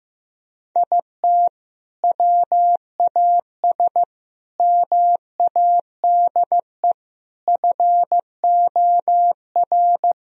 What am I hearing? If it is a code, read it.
Morse code, heard as a keyed tone, about 15 wpm, 700 Hz. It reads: IT WAS MADE FOR
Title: Sentences from Top 1000 Words - Review Part 4 - 15wpm